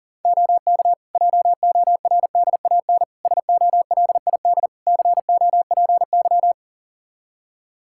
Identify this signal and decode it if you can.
Morse code, heard as a keyed tone, about 40 wpm, 700 Hz. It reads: OK JORDAN SOLID COPY
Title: Practice QSO - Exchange 3 - More Info - 40wpm